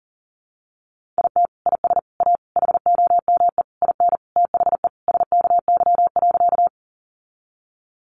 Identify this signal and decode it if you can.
Morse code, heard as a keyed tone, about 40 wpm, 700 Hz. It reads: IT IS A HOME IN THE SKY.
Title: Sentences from Top 1200 Words 40wpm